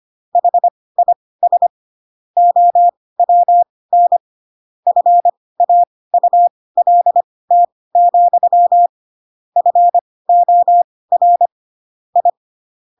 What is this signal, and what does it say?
Morse code, heard as a keyed tone, about 25 wpm, spaced as for 18 wpm, 700 Hz. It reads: HIS OWN FAULT, FOR I
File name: Pinocchio / Pinocchio ch7 12 25 ik1hge